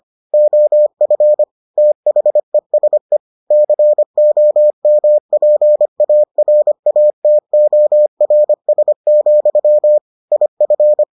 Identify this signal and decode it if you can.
Morse code, heard as a keyed tone, about 25 wpm, 600 Hz. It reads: OF THESE COMPARATORS, IF